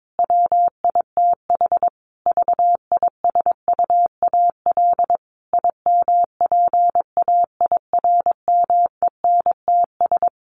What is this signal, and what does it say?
Morse code, heard as a keyed tone, about 22 wpm, 700 Hz. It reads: WITH VISUAL IMPAIRMENTS
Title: Sets of 5 Words Encore 22wpm